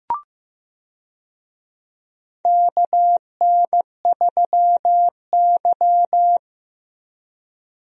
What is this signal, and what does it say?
Morse code, heard as a keyed tone, about 15 wpm, 700 Hz. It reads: KN3Y
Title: USA Call Signs - 4 Characters - Repeated 3x - 15wpm